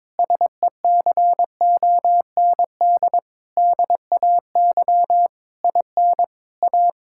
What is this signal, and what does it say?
Morse code, heard as a keyed tone, about 22 wpm, 700 Hz. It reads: SECOND DAY IN A
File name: Sentences with 7-Word Phrases - 22wpm